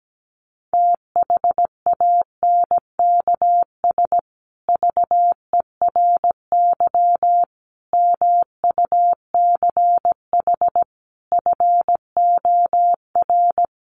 Morse code, heard as a keyed tone, about 17 wpm, 700 Hz. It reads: THANKS VERY MUCH FOR
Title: Sets of 5 Words Encore2 17wpm